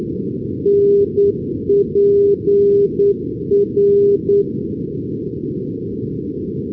Morse code, heard as a keyed tone, about 9 wpm, 400 Hz. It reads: NPR
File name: europe / pol / pol-415 np pol npr-488-2015